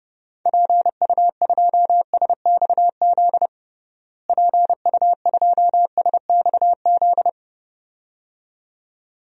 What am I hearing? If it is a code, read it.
Morse code, heard as a keyed tone, about 30 wpm, 700 Hz. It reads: PU2SXZ PU2SXZ
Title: DXCC-100 Call Signs - 6 Characters - Repeated 2x - 30wpm